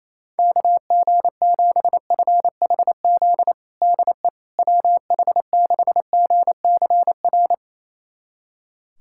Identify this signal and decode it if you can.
Morse code, heard as a keyed tone, about 28 wpm, 700 Hz. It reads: KG7FHZ DE WH6GCR
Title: USA Callsign DE Callsign - 6 Characters - 28wpm